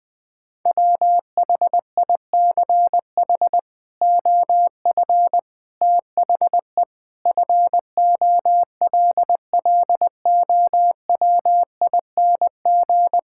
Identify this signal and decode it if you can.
Morse code, heard as a keyed tone, about 20 wpm, 700 Hz. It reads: WHICH OF THE FOLLOWING
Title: Sets of 8 Words - 20wpm